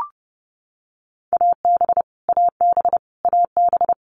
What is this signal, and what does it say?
Morse code, heard as a keyed tone, about 30 wpm, 700 Hz. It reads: AB AB AB
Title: Canadian Provinces and Territories - Repeated 3x - 30wpm